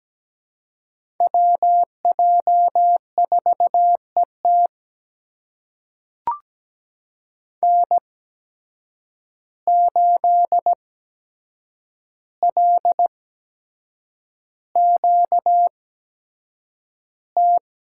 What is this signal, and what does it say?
Morse code, heard as a keyed tone, about 17 wpm, spaced as for 7 wpm, 700 Hz. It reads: WJ4ET N8LQT
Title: USA Call Signs - 5 Characters - 8x Character Spacing - 17wpm